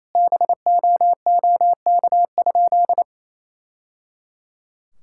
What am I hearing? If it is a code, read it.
Morse code, heard as a keyed tone, about 28 wpm, 700 Hz. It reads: BOOK?
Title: Sentences from Top 1000 Words - Review Part 6 - 28wpm